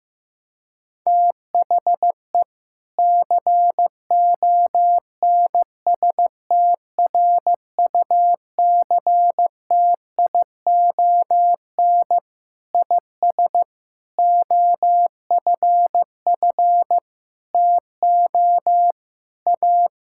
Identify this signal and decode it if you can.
Morse code, heard as a keyed tone, about 16 wpm, 700 Hz. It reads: THE CONSTRUCTION IS OFF TO A
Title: Sentences from Top 1600 Words 15wpm